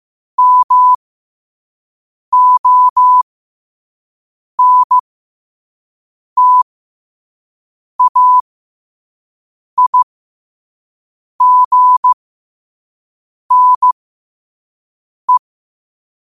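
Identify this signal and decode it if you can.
Morse code, heard as a keyed tone, about 15 wpm, spaced as for 5 wpm, 1000 Hz. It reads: MONTAIGNE
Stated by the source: Morse code quote of the day sent at 5 WPM.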